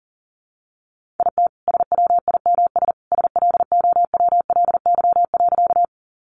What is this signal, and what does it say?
Morse code, heard as a keyed tone, about 40 wpm, 700 Hz. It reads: IT SWIMS SLOWLY.
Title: Easy Phrases 40wpm